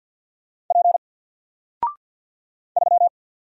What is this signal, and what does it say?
Morse code, heard as a keyed tone, about 48 wpm, 700 Hz. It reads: P 3
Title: Single Character (Letter:Number:Punctuation:Prosign) 50wpm